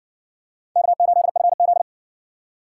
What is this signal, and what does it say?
Morse code, heard as a keyed tone, about 56 wpm, 700 Hz. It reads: K8FZ